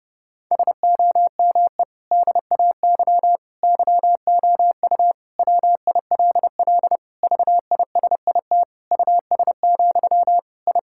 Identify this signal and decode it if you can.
Morse code, heard as a keyed tone, about 32 wpm, 700 Hz. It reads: SOME DAY YOU WILL VISIT US, I
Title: Sentences from Top 1000 Words - Review Part 6 - 30wpm